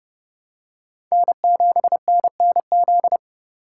Morse code, heard as a keyed tone, about 30 wpm, 700 Hz. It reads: N7NNZ
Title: USA Call Signs - 2x Character Spacing - 30wpm